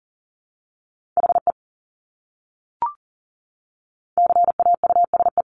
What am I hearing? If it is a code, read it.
Morse code, heard as a keyed tone, about 40 wpm, 700 Hz. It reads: HE CAUSE
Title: Top 200 Words Repeated 40wpm